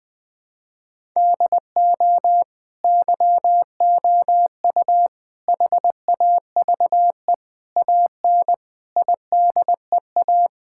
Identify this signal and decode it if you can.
Morse code, heard as a keyed tone, about 21 wpm, 700 Hz. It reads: DO YOU HAVE AN IDEA
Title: Sentences with 9-Word Phrases - 20wpm